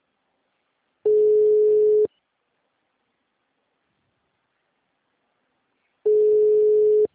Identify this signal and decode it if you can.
Ringback tone (Europe, 425 Hz).